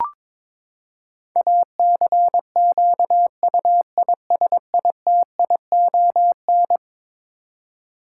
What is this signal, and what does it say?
Morse code, heard as a keyed tone, about 22 wpm, 700 Hz. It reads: ACQUISITION
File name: practice / 22wpm / Eleven Letter Words - 22wpm